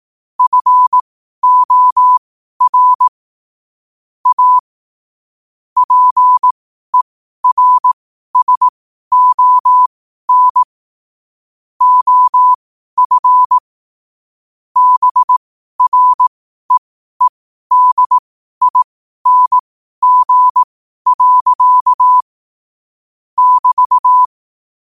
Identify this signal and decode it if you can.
Morse code, heard as a keyed tone, about 18 wpm, spaced as for 12 wpm, 1000 Hz. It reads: FOR A PERSON OF BREEDING. <BT>
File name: qotdfiles / 12wpm